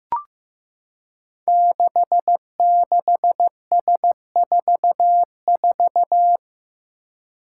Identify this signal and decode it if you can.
Morse code, heard as a keyed tone, about 15 wpm, 700 Hz. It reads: 66S44
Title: Five Challenging Letter-Number Combinations 15wpm